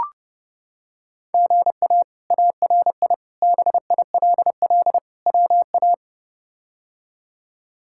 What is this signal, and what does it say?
Morse code, heard as a keyed tone, about 30 wpm, 700 Hz. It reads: GA ARI BILL WA
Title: K1USN SST QSO Practice - 30wpm